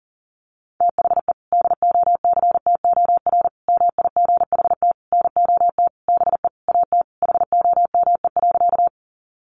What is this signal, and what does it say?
Morse code, heard as a keyed tone, about 40 wpm, 700 Hz. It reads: THE DOCTOR MIGHT NOT BE AT HOME.
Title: Sentences from Top 1500 Words 40wpm